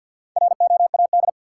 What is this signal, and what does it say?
Morse code, heard as a keyed tone, about 53 wpm, 700 Hz. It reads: ROAD
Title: Sentences from Top 400 Words - New Word Drill - 50wpm